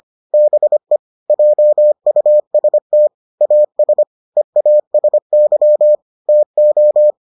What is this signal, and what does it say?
Morse code, heard as a keyed tone, about 25 wpm, 600 Hz. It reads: BE JUST AS EASY TO